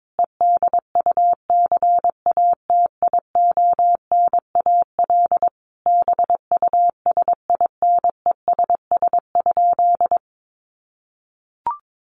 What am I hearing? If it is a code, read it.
Morse code, heard as a keyed tone, about 22 wpm, 700 Hz. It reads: EDUCATIONAL BUSINESS?
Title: Sentences from Top 1200 Words 22wpm